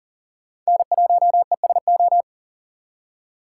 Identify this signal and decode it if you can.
Morse code, heard as a keyed tone, about 40 wpm, 700 Hz. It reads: N1ESO
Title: USA Call Signs - 5 Characters - 4x Character Spacing - 40wpm